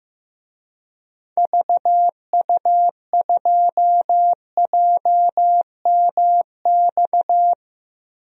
Morse code, heard as a keyed tone, about 15 wpm, 700 Hz. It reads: VU2JMX
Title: DXCC-100 Call Signs - 6 Characters - Repeated 3x - 15wpm